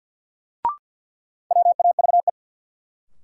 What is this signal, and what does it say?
Morse code, heard as a keyed tone, about 52 wpm, 700 Hz. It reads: WAVE
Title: Sentences from Top 1400 Words - New Word Drill - 50wpm